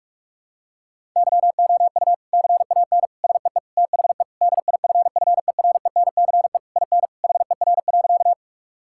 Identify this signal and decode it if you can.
Morse code, heard as a keyed tone, about 46 wpm, 700 Hz. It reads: YOU CAN SEE THE DIFFERENCE IN HER.
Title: Sentences from Top 1000 Words - Review Part 5 - 45wpm